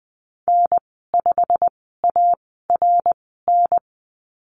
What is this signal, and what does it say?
Morse code, heard as a keyed tone, about 20 wpm, spaced as for 15 wpm, 700 Hz. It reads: N5ARN
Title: 200 Lesson 32g - Letter k - Call Signs 20wpm